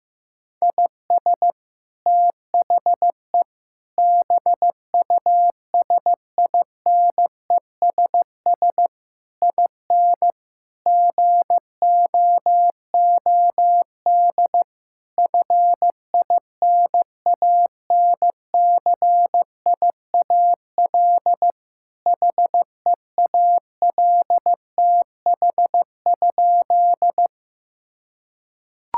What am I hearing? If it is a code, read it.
Morse code, heard as a keyed tone, about 16 wpm, 700 Hz. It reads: IS THE BUSINESS IN GOOD FINANCIAL HEALTH?
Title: Sentences from Top 1000 Words - Review Part 5 - 15wpm